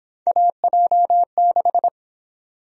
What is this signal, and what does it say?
Morse code, heard as a keyed tone, about 27 wpm, 700 Hz. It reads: AJ6
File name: Speed-Racing - Three Letter-Number Combinations - 22wpm